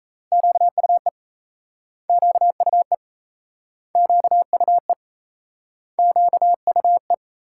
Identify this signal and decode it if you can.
Morse code, heard as a keyed tone, about 38 wpm, 700 Hz. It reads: QUE QUE QUE QUE